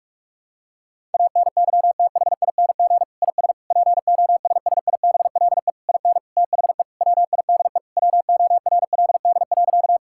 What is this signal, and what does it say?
Morse code, heard as a keyed tone, about 44 wpm, 700 Hz. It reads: ANYTHING IS POSSIBLE IN THE WIDE WORLD.
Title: Sentences from Top 1000 Words - Review Part 6 - 45wpm